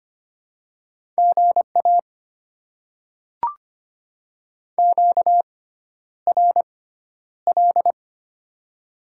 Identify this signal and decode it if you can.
Morse code, heard as a keyed tone, about 25 wpm, spaced as for 11 wpm, 690 Hz. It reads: GA QRL
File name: Top 90 QSO Elements - 6x Character Spacing - 25wpm